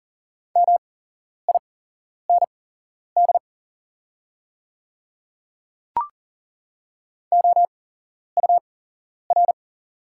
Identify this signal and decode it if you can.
Morse code, heard as a keyed tone, about 39 wpm, spaced as for 11 wpm, 690 Hz. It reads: MIND OUR
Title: 8x Character Spacing - Top 500 Words - 40wpm